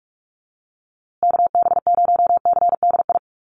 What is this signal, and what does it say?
Morse code, heard as a keyed tone, about 44 wpm, 700 Hz. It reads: KB0CDI
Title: USA Call Signs Repeated 2x 45wpm